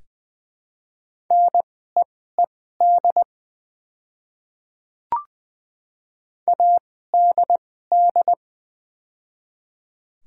Morse code, heard as a keyed tone, about 20 wpm, spaced as for 15 wpm, 700 Hz. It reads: NEED ADD
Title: 2x Character Spacing - Top 200 Words - 20wpm